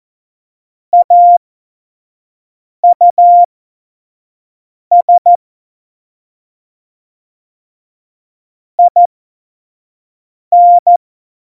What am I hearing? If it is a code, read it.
Morse code, heard as a keyed tone, about 14 wpm, spaced as for 5 wpm, 700 Hz. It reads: AUS IN
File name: Morsekurs / Witze / Witz 24.Dezember 2019, 25 F70